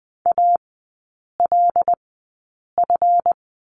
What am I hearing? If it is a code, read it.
Morse code, heard as a keyed tone, about 20 wpm, spaced as for 8 wpm, 700 Hz. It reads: ALF
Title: 138 Lesson 23e - Letter f - Three Characters 20wpm